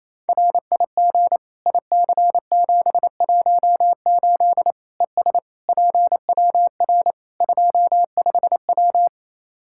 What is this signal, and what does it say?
Morse code, heard as a keyed tone, about 28 wpm, 700 Hz. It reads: RIG IC718 ES PWR 25W
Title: Practice QSO - Exchange 2 - Info - 28wpm